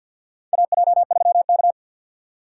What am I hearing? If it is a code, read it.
Morse code, heard as a keyed tone, about 50 wpm, 700 Hz. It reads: AJ3K
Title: USA Call Signs - 2x Character Spacing - 50wpm